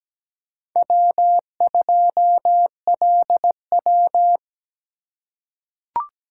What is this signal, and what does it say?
Morse code, heard as a keyed tone, about 17 wpm, 700 Hz. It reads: W2LW
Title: USA Call Signs - 4 Characters - 4x Character Spacing - 17wpm